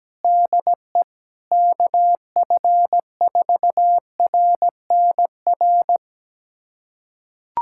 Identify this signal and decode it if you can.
Morse code, heard as a keyed tone, about 17 wpm, 700 Hz. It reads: DE KF4RNR
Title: USA Callsign DE Callsign - 6 Characters - 17wpm